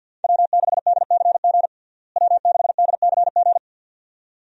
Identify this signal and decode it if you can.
Morse code, heard as a keyed tone, about 49 wpm, 700 Hz. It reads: W6DCG W6DCG
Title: USA Call Signs - 5 Characters - Repeated 2x - 50wpm